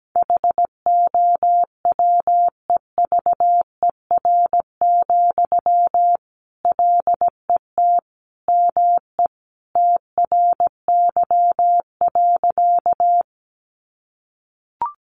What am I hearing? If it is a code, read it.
Morse code, heard as a keyed tone, about 17 wpm, 700 Hz. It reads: HOWEVER, LET ME TRY.
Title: Sentences from Top 600 Words 17wpm